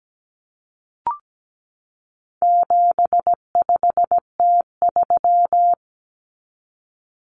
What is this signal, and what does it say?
Morse code, heard as a keyed tone, about 17 wpm, 700 Hz. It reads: 75T3
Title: Common Cut Numbers - 4-Digit 17wpm